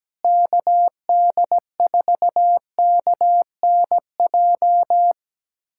Morse code, heard as a keyed tone, about 17 wpm, 700 Hz. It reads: KD4KNJ
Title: USA Callsign DE Callsign 17wpm